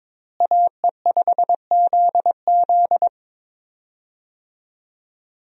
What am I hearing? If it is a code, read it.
Morse code, heard as a keyed tone, about 21 wpm, 700 Hz. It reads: AE5ZZ
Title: USA Call Signs - 5 Characters - Repeated 2x - 22wpm